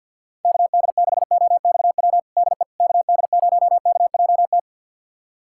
Morse code, heard as a keyed tone, about 47 wpm, 700 Hz. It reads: KD6OXW DE KD0KJT
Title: USA Callsign DE Callsign - 6 Characters - 50wpm